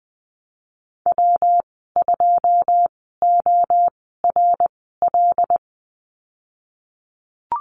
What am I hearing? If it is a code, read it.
Morse code, heard as a keyed tone, about 19 wpm, spaced as for 14 wpm, 700 Hz. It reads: W2ORL
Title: 147 Lesson 24g - Letter y - Call Signs 20wpm